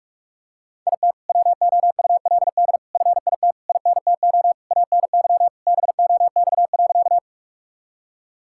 Morse code, heard as a keyed tone, about 46 wpm, 700 Hz. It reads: IT WOULD FIT INTO ANY BOX.
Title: Sentences from Top 1000 Words - Review Part 5 - 45wpm